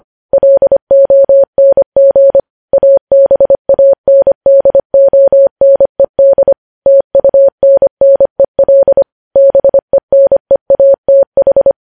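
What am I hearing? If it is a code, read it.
Morse code, heard as a keyed tone, about 25 wpm, 550 Hz. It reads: LONG ABANDONED TUNNEL BENEATH
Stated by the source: Morse code practice of news headlines at 25 wpm, opening with vvv.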